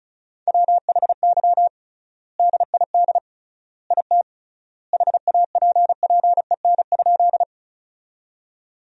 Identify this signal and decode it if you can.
Morse code, heard as a keyed tone, about 34 wpm, 700 Hz. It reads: WHY DID IT HAPPEN?
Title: 3x Word-Spacing - Sentences from Top 400 Words - 35wpm